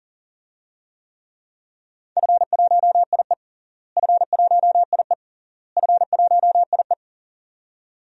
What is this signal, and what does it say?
Morse code, heard as a keyed tone, about 40 wpm, 700 Hz. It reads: F1IE F1IE F1IE
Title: DXCC-100 Call Signs - 4 Characters - Repeated 3x - 40wpm